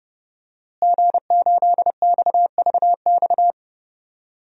Morse code, heard as a keyed tone, about 32 wpm, 700 Hz. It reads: G8XVX
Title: DXCC-100 Call Signs - 5 Characters - Repeated 2x - 30wpm